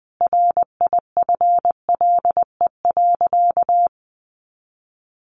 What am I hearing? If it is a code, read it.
Morse code, heard as a keyed tone, about 20 wpm, 700 Hz. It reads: RIFLE.
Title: A Sound Of Thunder 20wpm